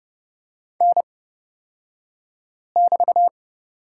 Morse code, heard as a keyed tone, about 30 wpm, spaced as for 5 wpm, 700 Hz. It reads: N<BT>
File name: Lesson 36d - BT - Two Characters 30wpm